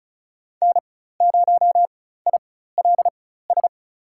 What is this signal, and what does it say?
Morse code, heard as a keyed tone, about 34 wpm, spaced as for 16 wpm, 700 Hz. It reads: N0ILS
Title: USA Call Signs - 4x Character Spacing - 35wpm